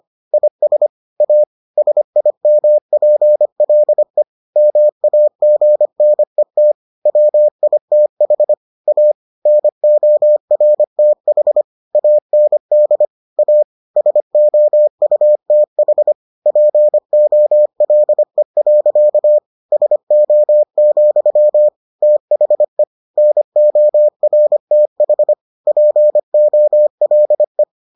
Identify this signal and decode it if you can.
Morse code, heard as a keyed tone, about 25 wpm, 600 Hz. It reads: IS A SIMPLE MAGNET WITH A NORTH AND A SOUTH POLE. SO, THE NORTH POLE